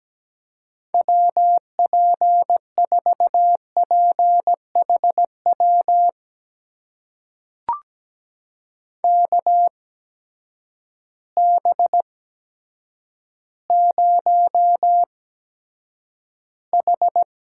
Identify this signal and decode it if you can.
Morse code, heard as a keyed tone, about 17 wpm, spaced as for 9 wpm, 700 Hz. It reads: WP4PHW KB0H
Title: USA Call Signs - 6 Characters - 8x Character Spacing - 17wpm